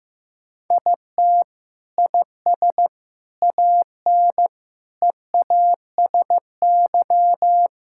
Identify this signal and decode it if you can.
Morse code, heard as a keyed tone, about 15 wpm, 700 Hz. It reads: IT IS AN EASY
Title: Sentences from Top 1000 Words - Review Part 6 - 15wpm